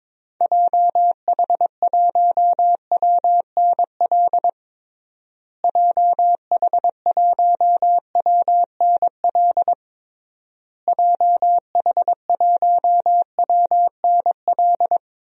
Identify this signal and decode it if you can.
Morse code, heard as a keyed tone, about 22 wpm, 700 Hz. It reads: JH1WNL JH1WNL JH1WNL
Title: DXCC-100 Call Signs - 6 Characters - Repeated 3x - 22wpm